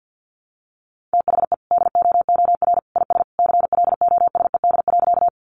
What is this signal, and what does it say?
Morse code, heard as a keyed tone, about 50 wpm, 700 Hz. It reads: THE DOOR IS CLOSED.
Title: Sentences from Top 1400 Words 50wpm